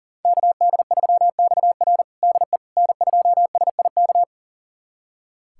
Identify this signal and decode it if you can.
Morse code, heard as a keyed tone, about 40 wpm, 700 Hz. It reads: KD3XR DE N2SIK
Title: USA Callsign DE Callsign - 5 Characters - 40wpm